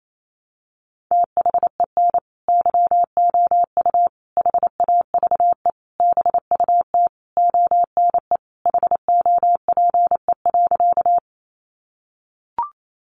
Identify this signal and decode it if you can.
Morse code, heard as a keyed tone, about 28 wpm, 700 Hz. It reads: THEN YOU HAVE BUT ONE HOPE.
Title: Sentences from Top 1000 Words 28wpm